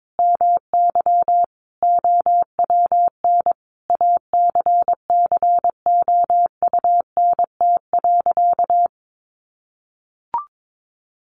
Morse code, heard as a keyed tone, about 22 wpm, 700 Hz. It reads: MY OWN ACCOUNT.
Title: Sentences from Top 1400 Words 22wpm